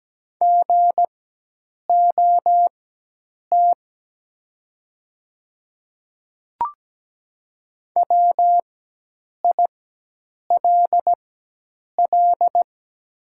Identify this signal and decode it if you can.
Morse code, heard as a keyed tone, about 17 wpm, spaced as for 8 wpm, 700 Hz. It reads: GOT WILL
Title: 4x Character Spacing - Top 300 Words - 17wpm